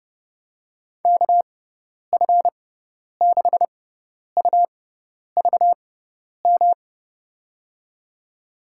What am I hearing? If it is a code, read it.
Morse code, heard as a keyed tone, about 31 wpm, spaced as for 10 wpm, 700 Hz. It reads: KF6UVM
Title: USA Call Signs - 6 Characters - 6x Character Spacing - 30wpm